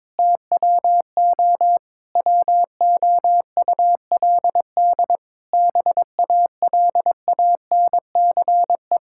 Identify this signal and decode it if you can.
Morse code, heard as a keyed tone, about 22 wpm, 700 Hz. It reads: TWO WOULD BALANCE
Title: Sentences from Top 1000 Words - Review Part 6 - 22wpm